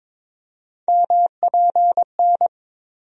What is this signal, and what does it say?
Morse code, heard as a keyed tone, about 22 wpm, 700 Hz. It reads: MPN
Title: USA Call Sign Postfixes - 3 Letters 22wpm